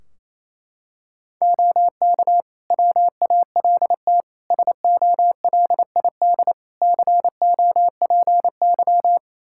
Morse code, heard as a keyed tone, about 27 wpm, 700 Hz. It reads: OK WALT SOLID COPY
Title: Practice QSO - Exchange 3 - More Info - 28wpm